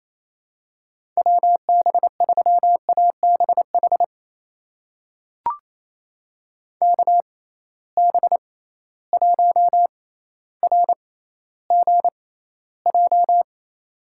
Morse code, heard as a keyed tone, about 28 wpm, spaced as for 14 wpm, 700 Hz. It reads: WB3ABH KB1RGJ
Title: USA Call Signs - 6 Characters - 6x Character Spacing - 28wpm